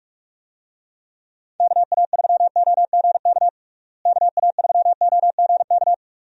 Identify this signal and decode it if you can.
Morse code, heard as a keyed tone, about 46 wpm, 700 Hz. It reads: KA3OGK KA3OGK
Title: USA Call Signs - 6 Characters - Repeated 3x - 45wpm